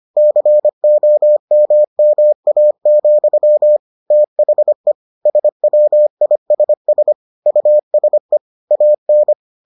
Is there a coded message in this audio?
Yes: Morse code from 0.2 to 9.3 s, heard as a keyed tone, about 25 wpm, 600 Hz. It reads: COMMA, THE SWISS USE AN